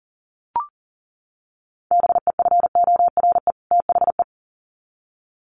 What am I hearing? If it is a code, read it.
Morse code, heard as a keyed tone, about 40 wpm, 700 Hz. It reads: BEFORE THE
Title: Sets of 2 Words 40wpm